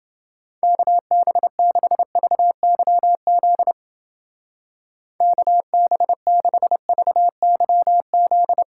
Morse code, heard as a keyed tone, about 31 wpm, 700 Hz. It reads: KB6VYZ KB6VYZ